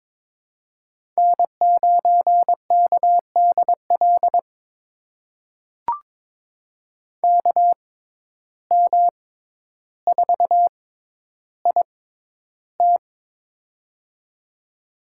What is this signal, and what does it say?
Morse code, heard as a keyed tone, about 22 wpm, spaced as for 11 wpm, 700 Hz. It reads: N9KDL KM4IT
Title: USA Call Signs - 5 Characters - 6x Character Spacing - 22wpm